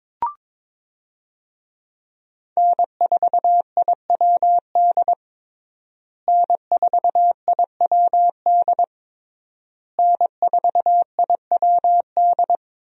Morse code, heard as a keyed tone, about 22 wpm, 700 Hz. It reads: N4IWD N4IWD N4IWD
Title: USA Call Signs - 5 Characters - Repeated 3x - 22wpm